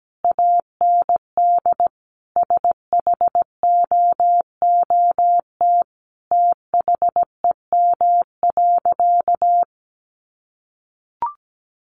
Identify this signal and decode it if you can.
Morse code, heard as a keyed tone, about 17 wpm, 700 Hz. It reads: AND SHOOT THEM.
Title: Sentences from Top 1200 Words 17wpm